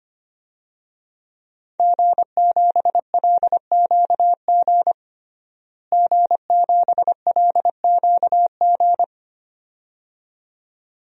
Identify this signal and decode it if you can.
Morse code, heard as a keyed tone, about 25 wpm, 700 Hz. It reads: G7LQG G7LQG
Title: DXCC-100 Call Signs - 5 Characters - Repeated 2x - 25wpm